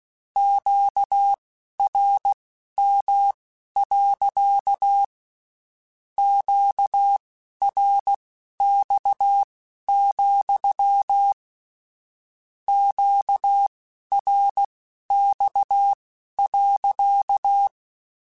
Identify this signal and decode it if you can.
Morse code, heard as a keyed tone, about 16 wpm, spaced as for 12 wpm, 800 Hz. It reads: QRM. QRX, QRX.
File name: Q Calls / Q Calls-mp3s / Q Calls sppc @15